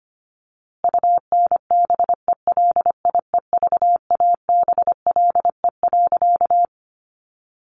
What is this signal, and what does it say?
Morse code, heard as a keyed tone, about 25 wpm, 700 Hz. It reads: UNBELIEVABLE.
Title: A Sound Of Thunder 25wpm